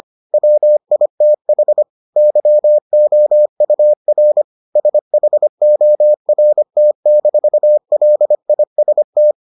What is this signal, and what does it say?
Morse code, heard as a keyed tone, about 25 wpm, 600 Hz. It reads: WITH YOUR SHORT-LIST